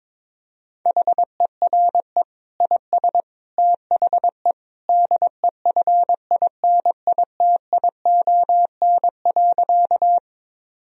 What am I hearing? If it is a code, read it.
Morse code, heard as a keyed tone, about 22 wpm, 700 Hz. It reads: HERE IS THE DEFINITION.
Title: Sentences from Top 1000 Words - Review Part 4 - 22wpm